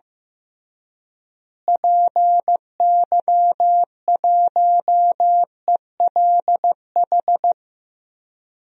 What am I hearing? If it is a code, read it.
Morse code, heard as a keyed tone, about 15 wpm, 700 Hz. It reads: PY1ELH
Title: DXCC-100 Call Signs - 6 Characters - Repeated 3x - 15wpm